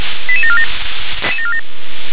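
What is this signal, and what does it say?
Five-tone selective call (ZVEI), heard as quick runs of five tones, tones of about 70 ms. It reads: 88747
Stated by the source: ZVEI five-tone selective call 88786, received via SDR.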